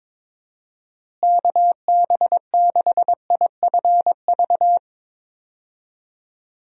Morse code, heard as a keyed tone, about 22 wpm, 700 Hz. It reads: KB6IFV
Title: USA Call Signs - 6 Characters - Repeated 3x - 22wpm